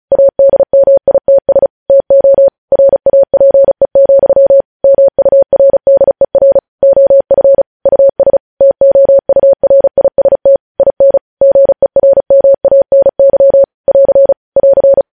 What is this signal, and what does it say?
Morse code, heard as a keyed tone, about 36 wpm, 550 Hz. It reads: ADMITS TO RAPE, MURDER OF US TOURIST IN GERMANY <AR> <AR>